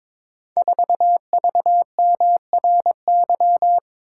Morse code, heard as a keyed tone, about 22 wpm, 690 Hz. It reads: 4VMRY
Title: Code Groups - 5 Letter:Numbers - 22wpm